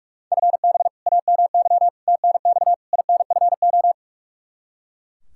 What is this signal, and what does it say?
Morse code, heard as a keyed tone, about 41 wpm, 700 Hz. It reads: FB AMY TNX INFO
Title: Practice QSO - Exchange 3 - More Info - 45wpm